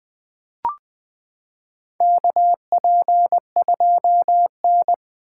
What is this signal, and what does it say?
Morse code, heard as a keyed tone, about 20 wpm, 700 Hz. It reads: KP2N
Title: K1USN SST QSO Practice - 20wpm